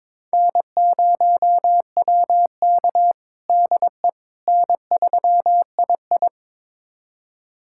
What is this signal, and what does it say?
Morse code, heard as a keyed tone, about 22 wpm, 700 Hz. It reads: N0WK DE N3II
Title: USA Callsign DE Callsign - 4 Characters - 22wpm